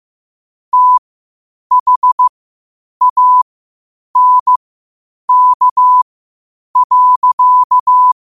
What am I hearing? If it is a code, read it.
Morse code, heard as a keyed tone, about 15 wpm, spaced as for 8 wpm, 1000 Hz. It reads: THANK.
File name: qotdfiles / 8wpm